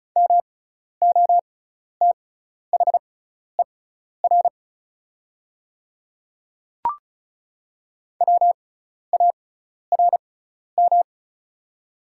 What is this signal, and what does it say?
Morse code, heard as a keyed tone, about 36 wpm, spaced as for 12 wpm, 700 Hz. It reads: MOTHER WARM
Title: 6x Character Spacing - Top 500 Words - 35wpm